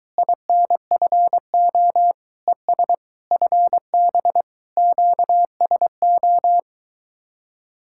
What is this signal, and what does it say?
Morse code, heard as a keyed tone, about 22 wpm, 700 Hz. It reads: INFO ES FB QSO